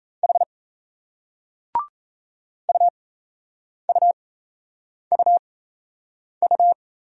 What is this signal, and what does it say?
Morse code, heard as a keyed tone, about 38 wpm, spaced as for 8 wpm, 700 Hz. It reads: H UUUU